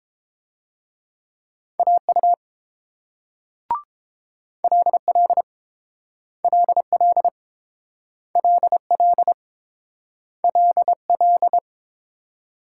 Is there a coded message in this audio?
Yes: Morse code from 1.8 to 11.6 s, heard as a keyed tone, about 31 wpm, 700 Hz. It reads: AU LL LL LL LL